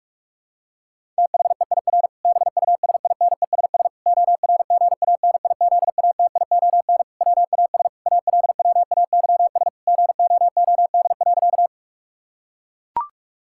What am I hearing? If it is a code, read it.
Morse code, heard as a keyed tone, about 41 wpm, 700 Hz. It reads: THEIR BUSINESS ORGANIZATION WAS ALWAYS GOOD.
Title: Sentences from Top 1000 Words - Review Part 5 - 45wpm